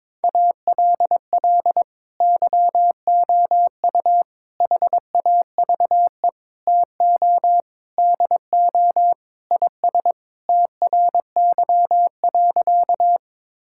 Morse code, heard as a keyed tone, about 21 wpm, 700 Hz. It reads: ALL YOU HAVE TO DO IS TRY.
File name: practice / 22wpm / Sentences with 3-Word Phrases - 22wpm